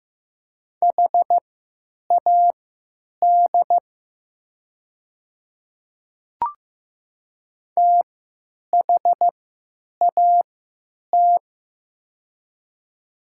Morse code, heard as a keyed tone, about 15 wpm, spaced as for 9 wpm, 700 Hz. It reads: HAD THAT
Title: 3x Character Spacing - Top 200 Words - 15wpm